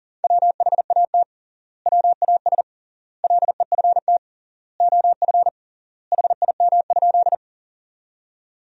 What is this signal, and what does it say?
Morse code, heard as a keyed tone, about 40 wpm, 700 Hz. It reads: WHAT WAS LEFT OF HIM?
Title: 3x Word-Spacing - Sentences from Top 300 Words - 40wpm